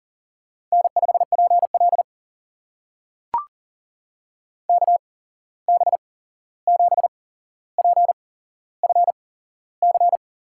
Morse code, heard as a keyed tone, about 40 wpm, spaced as for 15 wpm, 700 Hz. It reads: N5PL KB7PFC
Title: USA Call Signs - 8x Character Spacing - 40wpm